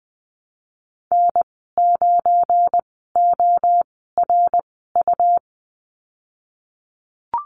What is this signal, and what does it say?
Morse code, heard as a keyed tone, about 20 wpm, spaced as for 15 wpm, 700 Hz. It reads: N9ORU
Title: 186 Lesson 30g - Letter b - Call Signs 20wpm